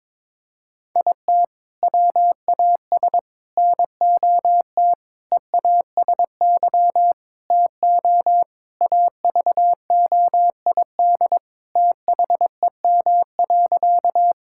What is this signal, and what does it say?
Morse code, heard as a keyed tone, about 22 wpm, 700 Hz. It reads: IT WAS NOT EASY TO AVOID THEM.
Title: Sentences from Top 1000 Words - Review Part 6 - 22wpm